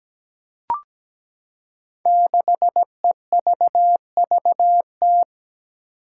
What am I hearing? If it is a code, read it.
Morse code, heard as a keyed tone, about 17 wpm, 700 Hz. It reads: 6EVVT
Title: Five Challenging Letter-Number Combinations 17wpm